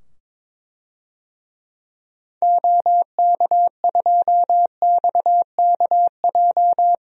Morse code, heard as a keyed tone, about 22 wpm, 700 Hz. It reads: OK2XKJ
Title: DXCC-100 Call Signs - 6 Characters - 22wpm